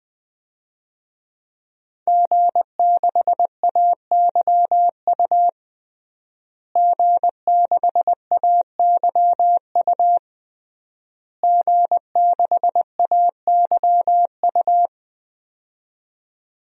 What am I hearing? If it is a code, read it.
Morse code, heard as a keyed tone, about 20 wpm, 700 Hz. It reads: G6AYU G6AYU G6AYU
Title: DXCC-100 Call Signs - 5 Characters - Repeated 3x - 20wpm